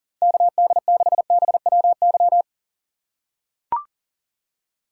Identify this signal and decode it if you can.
Morse code, heard as a keyed tone, about 40 wpm, 700 Hz. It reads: KD6BWY
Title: USA Call Signs - 6x Character Spacing - 40wpm